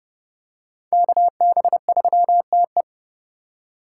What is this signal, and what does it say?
Morse code, heard as a keyed tone, about 29 wpm, 700 Hz. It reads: KB3TE